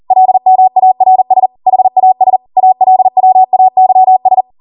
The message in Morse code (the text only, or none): LMARS HAS ALWAYS